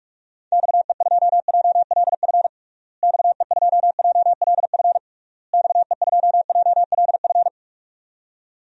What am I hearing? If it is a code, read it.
Morse code, heard as a keyed tone, about 47 wpm, 700 Hz. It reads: XE2JLF XE2JLF XE2JLF
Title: DXCC-100 Call Signs - 6 Characters - Repeated 3x - 45wpm